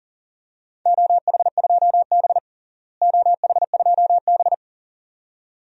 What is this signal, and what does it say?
Morse code, heard as a keyed tone, about 40 wpm, 700 Hz. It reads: OH2B OH2B
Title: DXCC-100 Call Signs - 4 Characters - Repeated 2x - 40wpm